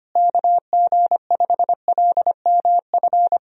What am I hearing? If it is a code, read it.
Morse code, heard as a keyed tone, about 25 wpm, 700 Hz. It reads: KG5LMF
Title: USA Call Signs - 6 Characters - Repeated 2x - 25wpm